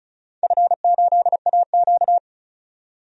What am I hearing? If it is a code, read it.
Morse code, heard as a keyed tone, about 36 wpm, 700 Hz. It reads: F8AQ
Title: DXCC-100 Call Signs - 4 Characters - 35wpm